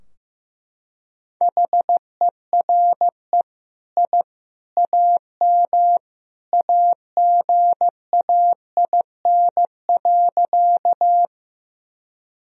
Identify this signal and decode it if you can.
Morse code, heard as a keyed tone, about 15 wpm, 700 Hz. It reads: HERE I AM AGAIN.
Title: Sentences from Top 1000 Words - Review Part 2 - 15wpm